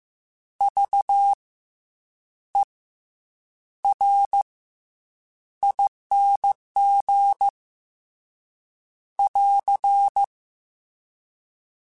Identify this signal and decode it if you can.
Morse code, heard as a keyed tone, about 15 wpm, 800 Hz. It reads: V E R ING <AR>
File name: Special INGs-mp3s / Special INGs-mp3s / Fast ing @13